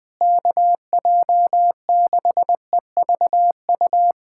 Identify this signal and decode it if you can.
Morse code, heard as a keyed tone, about 20 wpm, 700 Hz. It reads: KJ6EVU
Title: USA Callsign DE Callsign - 6 Characters - 20wpm